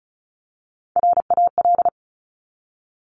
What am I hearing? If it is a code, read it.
Morse code, heard as a keyed tone, about 36 wpm, 700 Hz. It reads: RAL
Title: Top Three Letter Combinations in Words 35wpm